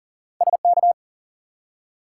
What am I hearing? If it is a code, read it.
Morse code, heard as a keyed tone, about 40 wpm, 700 Hz. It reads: SK
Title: US States and Canadian Provinces 40wpm